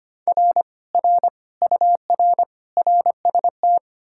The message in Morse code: R R UR RST

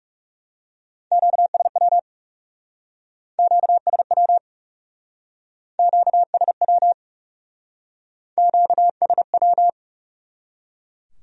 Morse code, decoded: QSW QSW QSW QSW

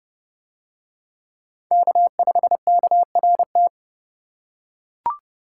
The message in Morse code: K5KRT

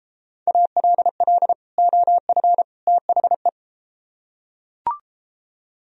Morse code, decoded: ALL OF THE

